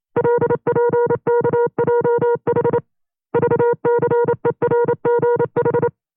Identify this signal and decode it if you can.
Morse code, heard as a keyed tone, about 28 wpm, 480 Hz. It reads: LPKJH VCERGH
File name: FreeFiles / CW-practice / 25WPM-side1